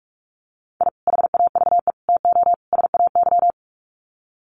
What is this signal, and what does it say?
Morse code, heard as a keyed tone, about 44 wpm, 700 Hz. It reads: I HAVE TO SAY
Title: Sets of 4 Words 45wpm